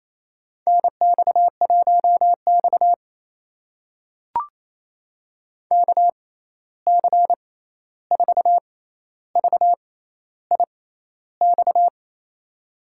Morse code, heard as a keyed tone, about 27 wpm, spaced as for 13 wpm, 700 Hz. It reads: NX1X KC4VIX
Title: USA Call Signs - 6x Character Spacing - 28wpm